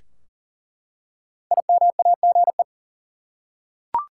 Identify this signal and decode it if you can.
Morse code, heard as a keyed tone, about 39 wpm, 700 Hz. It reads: IMAGE
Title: Five Letter Words - 40wpm